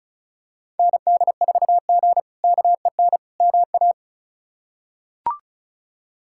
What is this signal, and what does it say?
Morse code, heard as a keyed tone, about 36 wpm, 700 Hz. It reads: ND4G KEN MA
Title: K1USN SST QSO Practice - 35wpm